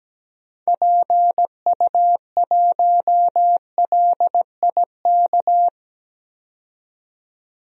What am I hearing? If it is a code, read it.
Morse code, heard as a keyed tone, about 17 wpm, 700 Hz. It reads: PU1LIK
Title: DXCC-100 Call Signs - 6 Characters - 17wpm